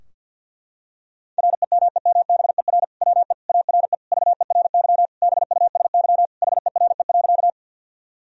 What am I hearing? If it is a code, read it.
Morse code, heard as a keyed tone, about 50 wpm, 700 Hz. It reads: REMEMBER WE ARE VERY BUSY HERE.
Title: Sentences from Top 1000 Words - Review Part 3 - 50wpm